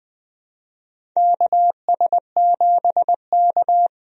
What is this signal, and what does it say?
Morse code, heard as a keyed tone, about 21 wpm, 700 Hz. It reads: KS7K
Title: USA Call Signs - 4 Characters - 20wpm